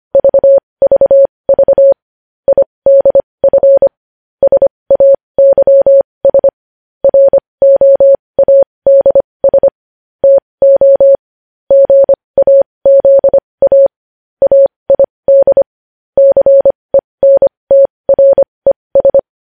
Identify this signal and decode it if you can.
Morse code, heard as a keyed tone, about 25 wpm, spaced as for 20 wpm, 550 Hz. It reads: VVV IDF SAYS ROADS TO GAZA AID CENTRES
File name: podcast / News headlines-1x-25x20